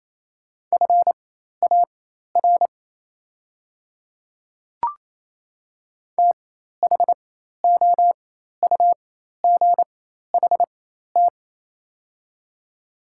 Morse code, heard as a keyed tone, about 28 wpm, spaced as for 13 wpm, 700 Hz. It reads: FAR THOUGHT
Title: 4x Character Spacing - Top 500 Words - 28wpm